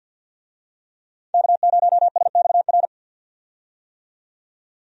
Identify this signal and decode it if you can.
Morse code, heard as a keyed tone, about 46 wpm, 700 Hz. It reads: K0SXR
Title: USA Call Signs - 5 Characters - 50wpm